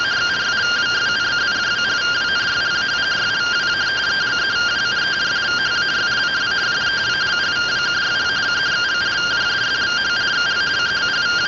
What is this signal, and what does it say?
Two-tone teleprinter signal, about 170 Hz shift.